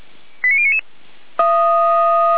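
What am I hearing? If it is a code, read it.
Five-tone selective call (ZVEI), heard as quick runs of five tones, tones of about 70 ms. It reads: 89099
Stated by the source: Five-tone ZVEI selective call received on an SDR.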